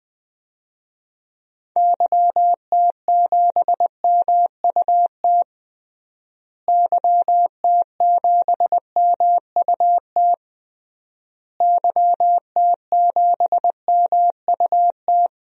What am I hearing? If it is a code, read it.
Morse code, heard as a keyed tone, about 20 wpm, 700 Hz. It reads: YT7MUT YT7MUT YT7MUT
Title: DXCC-100 Call Signs - 6 Characters - Repeated 3x - 20wpm